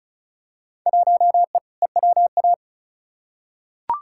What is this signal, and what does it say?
Morse code, heard as a keyed tone, about 36 wpm, 700 Hz. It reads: 1E EWA
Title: Field Day Exchanges - 35wpm